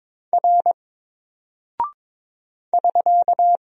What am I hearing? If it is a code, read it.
Morse code, heard as a keyed tone, about 22 wpm, 700 Hz. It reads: R <SK>
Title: Single Character (Letter:Number:Punctuation:Prosign) 22wpm